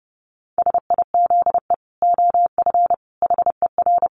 Morse code, heard as a keyed tone, about 30 wpm, 700 Hz. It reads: SIZE OF HER
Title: Dit Challenge 30wpm